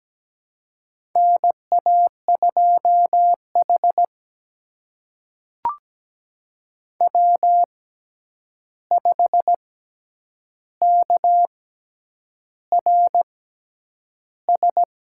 Morse code, heard as a keyed tone, about 17 wpm, spaced as for 8 wpm, 700 Hz. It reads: NA2H W5KRS
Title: USA Call Signs - 6x Character Spacing - 17wpm